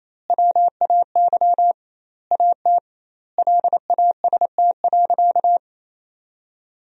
Morse code, heard as a keyed tone, about 28 wpm, 700 Hz. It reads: WAY AT LAST.
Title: 2x Word-Spacing - Sentences from Top 300 Words - 28wpm